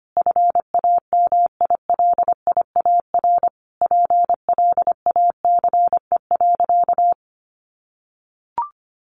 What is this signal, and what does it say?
Morse code, heard as a keyed tone, about 25 wpm, 700 Hz. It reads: FAMILIAR PLACE.
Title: Sentences from Top 1500 Words 25wpm